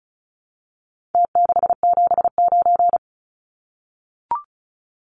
Morse code, heard as a keyed tone, about 34 wpm, 700 Hz. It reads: T679
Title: Numbers - Contest Style - 4-Digit 35wpm